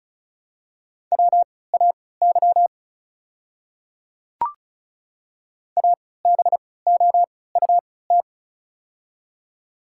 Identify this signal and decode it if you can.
Morse code, heard as a keyed tone, about 36 wpm, spaced as for 20 wpm, 700 Hz. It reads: WAY ABOUT
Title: 3x Character Spacing - Top 200 Words - 35wpm